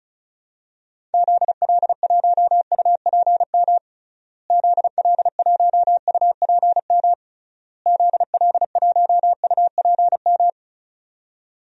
Morse code, heard as a keyed tone, about 34 wpm, 700 Hz. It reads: ZL1UPM ZL1UPM ZL1UPM
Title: DXCC-100 Call Signs - 6 Characters - Repeated 3x - 35wpm